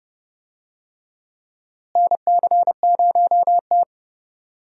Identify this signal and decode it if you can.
Morse code, heard as a keyed tone, about 31 wpm, 700 Hz. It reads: NC0T
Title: USA Call Signs - 4 Characters - Repeated 3x - 30wpm